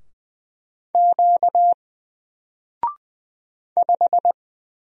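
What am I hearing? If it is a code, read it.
Morse code, heard as a keyed tone, about 20 wpm, 700 Hz. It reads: Q 5
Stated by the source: Single letter and number Morse drill at 20 wpm, from N4LCD.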